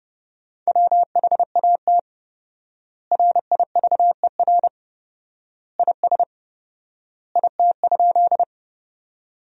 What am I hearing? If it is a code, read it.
Morse code, heard as a keyed tone, about 30 wpm, 700 Hz. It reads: WHAT RIVER IS IT?
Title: 4x Word-Spacing - Sentences from Top 300 Words - 30wpm